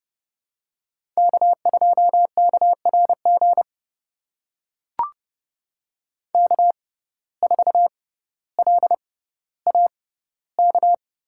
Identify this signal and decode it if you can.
Morse code, heard as a keyed tone, about 30 wpm, spaced as for 15 wpm, 700 Hz. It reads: K2KRG K4LAK
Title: USA Call Signs - 5 Characters - 6x Character Spacing - 30wpm